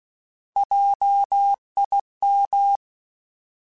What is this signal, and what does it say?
Morse code, heard as a keyed tone, about 16 wpm, 800 Hz. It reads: JIM
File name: Alphabetical / Alphabetical mp3s / Js @15